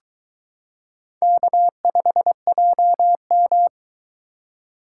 Morse code, heard as a keyed tone, about 22 wpm, 700 Hz. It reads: K5JM